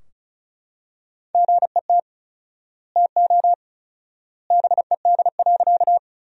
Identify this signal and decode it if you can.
Morse code, heard as a keyed tone, about 34 wpm, 700 Hz. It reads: GET TO BED.
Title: 4x Word-Spacing - Sentences from Top 500 Words - 35wpm